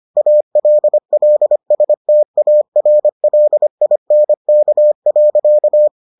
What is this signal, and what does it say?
Morse code, heard as a keyed tone, about 25 wpm, 600 Hz. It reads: ALLSTARLINK.